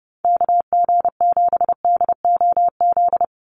Morse code, heard as a keyed tone, about 30 wpm, 700 Hz. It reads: KG7DOZ
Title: USA Call Signs 30wpm